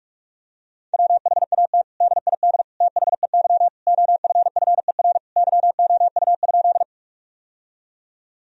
Morse code, heard as a keyed tone, about 41 wpm, 700 Hz. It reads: WHAT DID THEY OFFER YOU?
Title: Sentences from Top 1000 Words - Review Part 4 - 45wpm